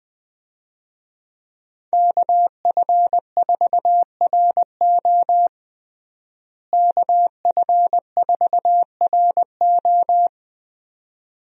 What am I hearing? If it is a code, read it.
Morse code, heard as a keyed tone, about 20 wpm, 700 Hz. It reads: KF4RO KF4RO
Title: USA Call Signs - 5 Characters - Repeated 2x - 20wpm